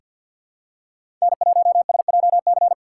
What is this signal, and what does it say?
Morse code, heard as a keyed tone, about 49 wpm, 700 Hz. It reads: N1SJC
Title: USA Call Signs - 5 Characters - 6x Character Spacing - 50wpm